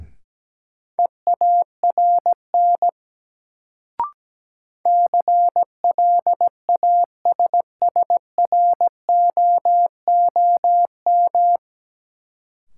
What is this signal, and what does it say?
Morse code, heard as a keyed tone, about 17 wpm, 700 Hz. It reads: EARN CLASSROOM
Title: Sentences from Top 1500 Words - New Word Drill Alt - 17wpm